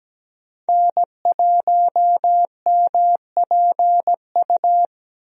Morse code, heard as a keyed tone, about 17 wpm, 700 Hz. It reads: N1MPU
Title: USA Call Signs - 5 Characters - 17wpm